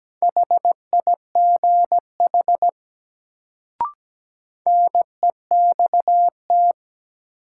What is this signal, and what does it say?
Morse code, heard as a keyed tone, about 17 wpm, 700 Hz. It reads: HIGH NEXT
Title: Sets of 2 Words - New Word Drill - 17wpm